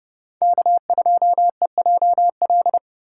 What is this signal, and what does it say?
Morse code, heard as a keyed tone, about 29 wpm, 700 Hz. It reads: K2EJL